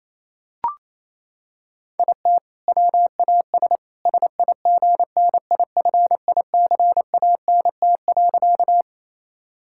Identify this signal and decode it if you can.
Morse code, heard as a keyed tone, about 28 wpm, 700 Hz. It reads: IT WAS SIGNIFICANT.
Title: Sentences from Top 1000 Words - Review Part 6 - 28wpm